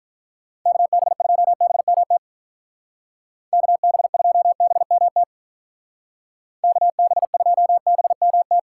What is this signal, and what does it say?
Morse code, heard as a keyed tone, about 44 wpm, 700 Hz. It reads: KB2BMT KB2BMT KB2BMT
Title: Speed Racing - USA 2x3 Call Signs 35wpm